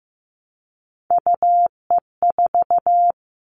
Morse code, heard as a keyed tone, about 15 wpm, 700 Hz. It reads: UE4
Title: Three Challenging Letter-Number Combinations 15wpm